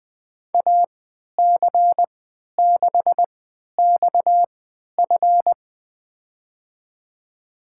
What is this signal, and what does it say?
Morse code, heard as a keyed tone, about 20 wpm, spaced as for 11 wpm, 700 Hz. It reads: AC6XF
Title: USA Call Signs - 5 Characters - 3x Character Spacing - 20wpm